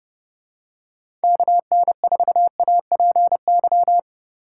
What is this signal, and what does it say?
Morse code, heard as a keyed tone, about 30 wpm, 700 Hz. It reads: KN4APY
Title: USA Call Signs - 2x Character Spacing - 30wpm